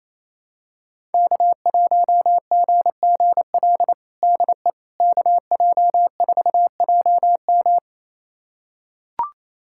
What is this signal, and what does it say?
Morse code, heard as a keyed tone, about 28 wpm, 700 Hz. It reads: K1GGL DE KJ4JM
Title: USA Callsign DE Callsign - 5 Characters - 28wpm